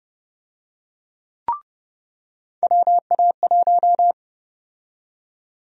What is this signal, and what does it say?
Morse code, heard as a keyed tone, about 30 wpm, 700 Hz. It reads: WA1
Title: USA Call Sign Prefixes - 30wpm